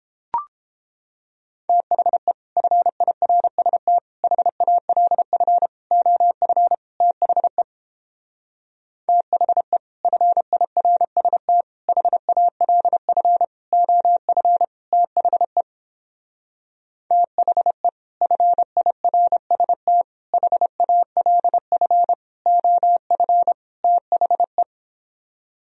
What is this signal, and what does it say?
Morse code, heard as a keyed tone, about 29 wpm, 700 Hz. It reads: THE FIRST HALF OF THE THE FIRST HALF OF THE THE FIRST HALF OF THE